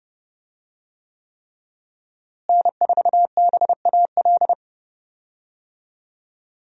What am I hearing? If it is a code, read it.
Morse code, heard as a keyed tone, about 30 wpm, 700 Hz. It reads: N4BAL
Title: USA Call Signs - 5 Characters - Repeated 3x - 30wpm